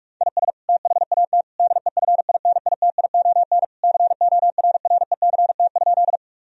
Morse code, heard as a keyed tone, about 46 wpm, 700 Hz. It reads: IS THAT DEFINITION CORRECT?
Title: Sentences from Top 1000 Words - Review Part 4 - 45wpm